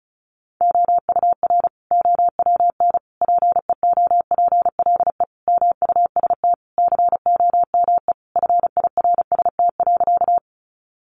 Morse code, heard as a keyed tone, about 34 wpm, 700 Hz. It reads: OUR OWN PEOPLE MUST COME FIRST.
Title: Sentences from Top 300 Words 35wpm